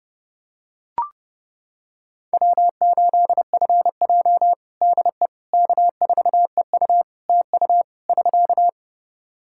Morse code, heard as a keyed tone, about 30 wpm, 700 Hz. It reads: W8FJ DE K4EU TU <SK>
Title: Practice QSO - Exchange 4 - Ending - 30wpm